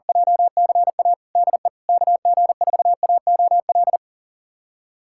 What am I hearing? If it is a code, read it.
Morse code, heard as a keyed tone, about 40 wpm, 700 Hz. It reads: JCA DE KG4AOL
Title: USA Callsign DE Callsign - 6 Characters - 40wpm